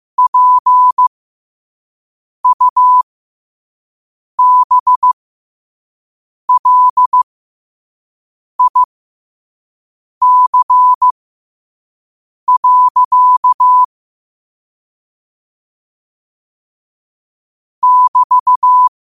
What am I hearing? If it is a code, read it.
Morse code, heard as a keyed tone, about 15 wpm, spaced as for 5 wpm, 1000 Hz. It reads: PUBLIC. <BT>
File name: qotdfiles / 5wpm